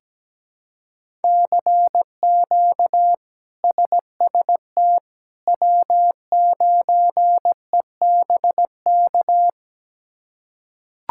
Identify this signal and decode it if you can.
Morse code, heard as a keyed tone, about 17 wpm, 700 Hz. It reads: CQ SST W9EBK
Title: K1USN SST QSO Practice - 17wpm